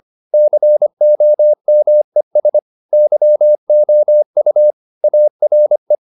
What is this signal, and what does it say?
Morse code, heard as a keyed tone, about 25 wpm, 600 Hz. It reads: COMES YOU ARE
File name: the old man and the sea part 1 25wpm